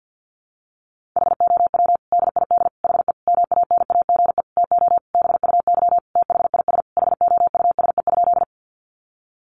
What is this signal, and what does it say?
Morse code, heard as a keyed tone, about 48 wpm, 700 Hz. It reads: HOW DID HE MANAGE TO BUY THIS HOUSE?
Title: Sentences from Top 1400 Words 50wpm